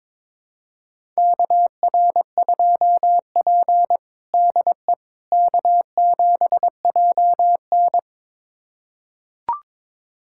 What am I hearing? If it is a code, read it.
Morse code, heard as a keyed tone, about 22 wpm, 700 Hz. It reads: KR2P DE K7JN
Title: USA Callsign DE Callsign - 4 Characters - 22wpm